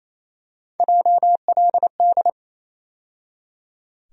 Morse code, heard as a keyed tone, about 28 wpm, 700 Hz. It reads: JLD
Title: USA Call Sign Postfixes - 3 Letters 28wpm